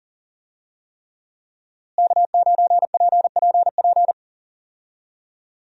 Morse code, heard as a keyed tone, about 40 wpm, 700 Hz. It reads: K9PPP
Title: USA Call Signs - 5 Characters - Repeated 3x - 40wpm